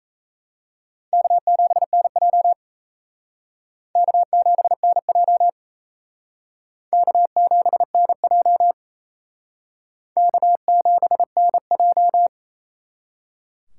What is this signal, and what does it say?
Morse code, heard as a keyed tone, about 35 wpm, 700 Hz. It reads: K7NJ K7NJ K7NJ K7NJ
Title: Speed Racing - USA 1x2 Call Signs 28wpm